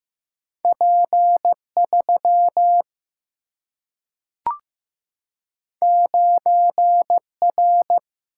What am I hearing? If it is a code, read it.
Morse code, heard as a keyed tone, about 15 wpm, 700 Hz. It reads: P 3 9 R
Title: Single-Letter Single-Number 15wpm